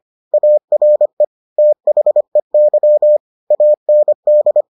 Morse code, heard as a keyed tone, about 25 wpm, 600 Hz. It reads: ARE THEY AND